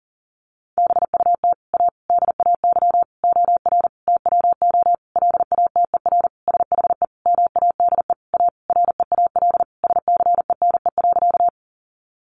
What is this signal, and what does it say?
Morse code, heard as a keyed tone, about 40 wpm, 700 Hz. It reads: BUT A DAY OR TWO LATER SHE MADE A REAL SCENE.
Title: Sentences from Top 1100 Words 40wpm